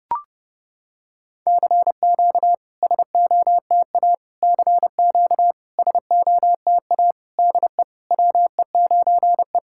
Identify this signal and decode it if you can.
Morse code, heard as a keyed tone, about 32 wpm, 700 Hz. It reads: CQ SOTA CQ SOTA DE WE9E
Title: SOTA QSO Practice 30wpm